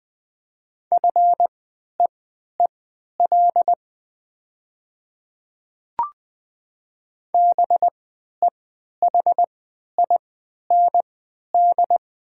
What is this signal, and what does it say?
Morse code, heard as a keyed tone, about 20 wpm, spaced as for 11 wpm, 700 Hz. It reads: FEEL BEHIND
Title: 3x Character Spacing - Top 500 Words - 20wpm